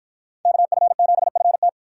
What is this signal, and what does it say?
Morse code, heard as a keyed tone, about 53 wpm, 700 Hz. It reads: KF7FT